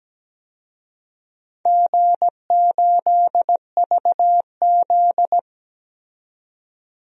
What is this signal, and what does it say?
Morse code, heard as a keyed tone, about 17 wpm, 700 Hz. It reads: G8VZ
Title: DXCC-100 Call Signs - 4 Characters - 17wpm